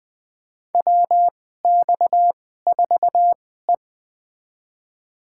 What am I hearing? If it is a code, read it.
Morse code, heard as a keyed tone, about 20 wpm, spaced as for 14 wpm, 700 Hz. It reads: WX4E
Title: USA Call Signs - 4 Characters - 2x Character Spacing - 20wpm